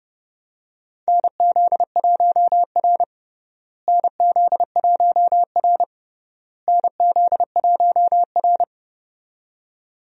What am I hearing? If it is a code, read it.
Morse code, heard as a keyed tone, about 30 wpm, 700 Hz. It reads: NZ1R NZ1R NZ1R
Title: USA Call Signs - 4 Characters - Repeated 3x - 30wpm